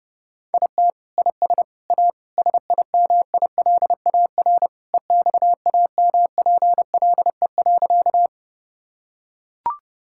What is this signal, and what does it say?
Morse code, heard as a keyed tone, about 30 wpm, 700 Hz. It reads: IT IS A SIMILAR EXAMPLE.
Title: Sentences from Top 1000 Words - Review Part 5 - 30wpm